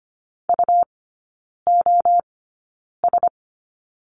Morse code, heard as a keyed tone, about 25 wpm, spaced as for 9 wpm, 700 Hz. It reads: UOS
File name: course / 25:10wpm / 100 Lesson 17e - Letter u - Three Characters 25wpm